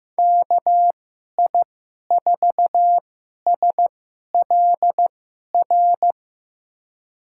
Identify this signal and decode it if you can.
Morse code, heard as a keyed tone, about 15 wpm, spaced as for 11 wpm, 700 Hz. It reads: KI4SLR
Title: USA Call Signs - 6 Characters - 2x Character Spacing - 15wpm